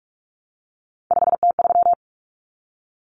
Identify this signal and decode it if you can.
Morse code, heard as a keyed tone, about 44 wpm, 700 Hz. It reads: 5T3
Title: Common Cut Numbers - 3-Digit 45wpm